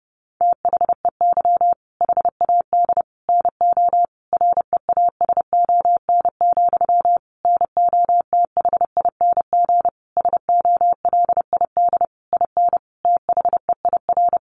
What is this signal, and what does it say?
Morse code, heard as a keyed tone, about 30 wpm, 700 Hz. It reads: THEY HAD NO REASON, NOTHING SOLID IN THEIR
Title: Sentences from Top 1500 Words 30wpm